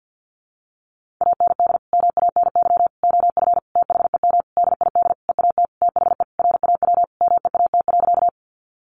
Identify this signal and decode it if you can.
Morse code, heard as a keyed tone, about 50 wpm, 700 Hz. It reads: AND MANY OF THEM DID EAT THE RAW MEAT.
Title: Sentences from Top 1500 Words 50wpm